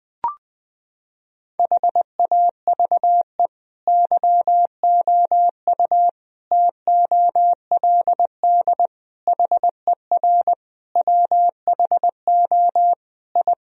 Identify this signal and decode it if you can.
Morse code, heard as a keyed tone, about 20 wpm, 700 Hz. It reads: HAVE YOU TOLD HER WHO I
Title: Sentences from Top 1000 Words - Review Part 2 - 20wpm